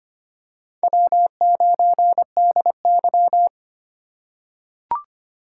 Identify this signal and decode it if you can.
Morse code, heard as a keyed tone, about 25 wpm, 700 Hz. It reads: W9DY
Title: K1USN SST QSO Practice - 25wpm